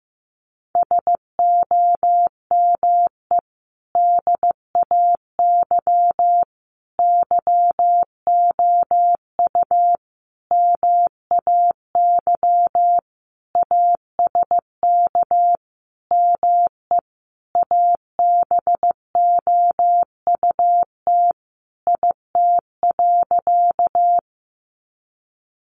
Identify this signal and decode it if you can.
Morse code, heard as a keyed tone, about 15 wpm, 700 Hz. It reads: SOME DAY YOU MAY ASK ME ABOUT IT.
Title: Sentences from Top 200 Words 15wpm